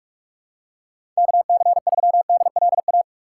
Morse code, heard as a keyed tone, about 44 wpm, 700 Hz. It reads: KC3DLA
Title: USA Call Signs - 6 Characters - 6x Character Spacing - 45wpm